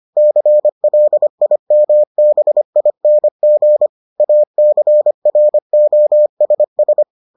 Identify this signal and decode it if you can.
Morse code, heard as a keyed tone, about 25 wpm, 600 Hz. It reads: CLIMBING ACROSS